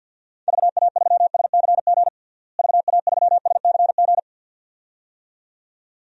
Morse code, heard as a keyed tone, about 53 wpm, 700 Hz. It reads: VA3SCG VA3SCG
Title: DXCC-100 Call Signs - 6 Characters - Repeated 2x - 50wpm